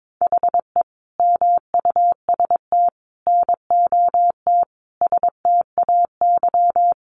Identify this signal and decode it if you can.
Morse code, heard as a keyed tone, about 22 wpm, 700 Hz. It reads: HE MUST NOT STAY
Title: Sentences from Top 600 Words 22wpm